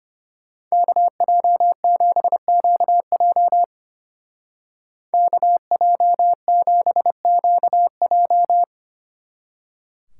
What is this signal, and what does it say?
Morse code, heard as a keyed tone, about 30 wpm, 700 Hz. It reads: KJ7QJ KJ7QJ
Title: Speed Racing - USA Call Signs 25wpm